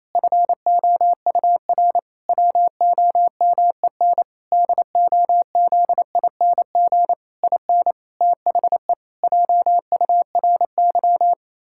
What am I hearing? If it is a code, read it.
Morse code, heard as a keyed tone, about 28 wpm, 700 Hz. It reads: FOUR WOMEN DOZING IN THE JURY
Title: Pangram Challenge v2 - 28wpm